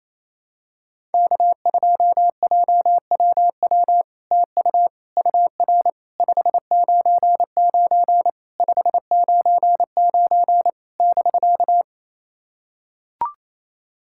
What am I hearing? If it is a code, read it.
Morse code, heard as a keyed tone, about 28 wpm, 700 Hz. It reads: K2JWW TU UR 599 599 <BK>
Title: SOTA QSO Practice 28wpm